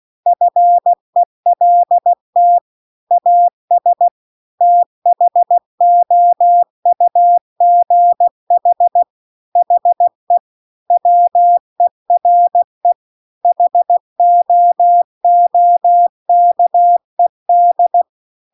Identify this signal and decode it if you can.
Morse code, heard as a keyed tone, about 16 wpm, 700 Hz. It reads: FELT AS THOUGH HE WERE HOOKED